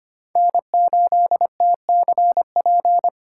Morse code, heard as a keyed tone, about 25 wpm, 700 Hz. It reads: N8TCP
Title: USA Call Signs - 5 Characters - 6x Character Spacing - 25wpm